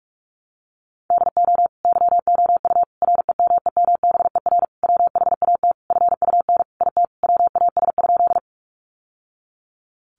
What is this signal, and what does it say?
Morse code, heard as a keyed tone, about 44 wpm, 700 Hz. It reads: DO YOU REMEMBER WHAT FUN IT WAS?
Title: Sentences from Top 700 Words 45wpm